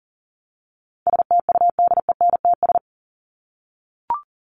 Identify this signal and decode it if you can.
Morse code, heard as a keyed tone, about 38 wpm, 700 Hz. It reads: STUDENTS
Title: ES:S Suffix - Top 100 Words with Base Words 40wpm